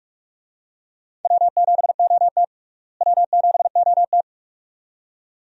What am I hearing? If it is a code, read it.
Morse code, heard as a keyed tone, about 46 wpm, 700 Hz. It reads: W7OT W7OT
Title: USA Call Signs - 4 Characters - Repeated 2x - 45wpm